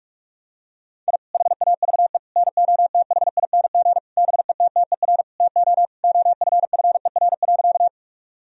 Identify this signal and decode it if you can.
Morse code, heard as a keyed tone, about 41 wpm, 700 Hz. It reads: I HAVE NOTHING BETTER TO OFFER.
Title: Sentences from Top 1000 Words - Review Part 4 - 45wpm